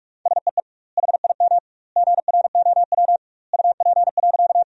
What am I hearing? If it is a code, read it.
Morse code, heard as a keyed tone, about 46 wpm, 700 Hz. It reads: SEE HIM GROW UP.
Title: 2x Word-Spacing - Sentences from Top 300 Words - 45wpm